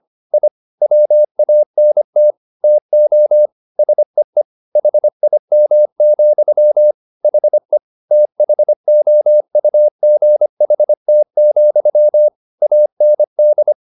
Morse code, heard as a keyed tone, about 25 wpm, 600 Hz. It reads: I WANT TO SEE HIM, HE THOUGHT, AND